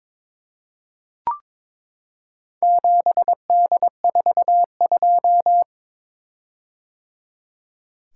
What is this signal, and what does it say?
Morse code, heard as a keyed tone, about 22 wpm, 700 Hz. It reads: 7D42
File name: practice / 22wpm / Four Letter-Number Combinations - 22wpm